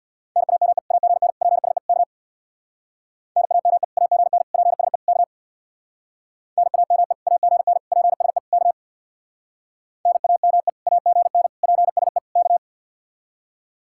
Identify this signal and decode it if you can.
Morse code, heard as a keyed tone, about 64 wpm, 700 Hz. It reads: NAME AGN PSE K NAME AGN PSE K NAME AGN PSE K NAME AGN PSE K